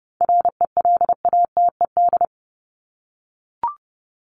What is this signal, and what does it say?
Morse code, heard as a keyed tone, about 30 wpm, 700 Hz. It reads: RELATED
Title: ED Suffix - Top 100 Words with Base Words 30wpm